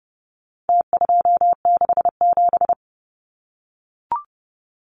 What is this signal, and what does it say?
Morse code, heard as a keyed tone, about 30 wpm, 700 Hz. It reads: T267
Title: Numbers - Contest Style - 4-Digit 30wpm